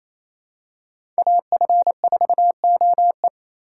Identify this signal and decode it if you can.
Morse code, heard as a keyed tone, about 28 wpm, 700 Hz. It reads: AF4OE
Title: USA Call Signs - 4x Character Spacing - 28wpm